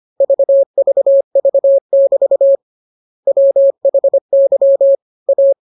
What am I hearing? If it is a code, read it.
Morse code, heard as a keyed tone, about 25 wpm, 550 Hz. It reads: VVV<BT> WHY A